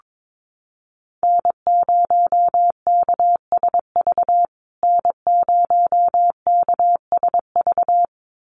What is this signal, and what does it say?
Morse code, heard as a keyed tone, about 22 wpm, 700 Hz. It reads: N0KSV N0KSV
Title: USA Call Signs Repeated 2x 22wpm